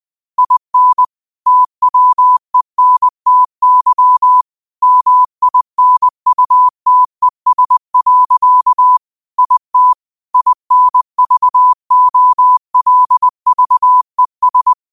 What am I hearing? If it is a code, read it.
Morse code, heard as a keyed tone, about 20 wpm, 1000 Hz. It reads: IN TWENTY MINUTES. IT INVOLVES
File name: qotdfiles / 20wpm